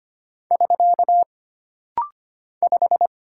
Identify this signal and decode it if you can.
Morse code, heard as a keyed tone, about 25 wpm, 700 Hz. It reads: <SK> 5
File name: Single Character (Letter:Number:Punctuation:Prosign) - Rapid-Fire - 25wpm